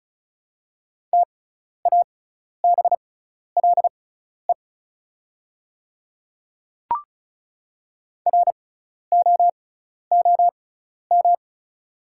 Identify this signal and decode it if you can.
Morse code, heard as a keyed tone, about 34 wpm, spaced as for 12 wpm, 690 Hz. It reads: TABLE ROOM
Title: 6x Character Spacing - Top 500 Words - 35wpm